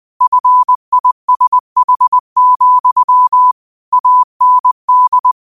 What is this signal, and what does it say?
Morse code, heard as a keyed tone, about 20 wpm, 1000 Hz. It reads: FISH, AND
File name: qotdfiles / 20wpm